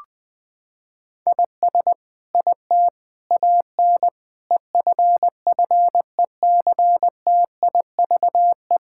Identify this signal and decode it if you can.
Morse code, heard as a keyed tone, about 21 wpm, 700 Hz. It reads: IS IT AN EFFECTIVE
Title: Sentences from Top 1000 Words - Review Part 6 - 20wpm